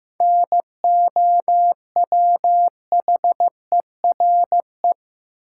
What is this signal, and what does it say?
Morse code, heard as a keyed tone, about 15 wpm, 700 Hz. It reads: NOWHERE
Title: Seven Letter Words - 15wpm